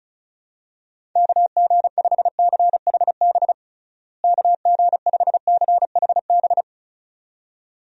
Morse code, heard as a keyed tone, about 36 wpm, 700 Hz. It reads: KG5CHB KG5CHB
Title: USA Call Signs - 6 Characters - Repeated 2x - 35wpm